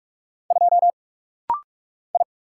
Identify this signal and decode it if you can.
Morse code, heard as a keyed tone, about 44 wpm, 700 Hz. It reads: 2 I
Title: Single Letter-Number - Rapid-Fire 45wpm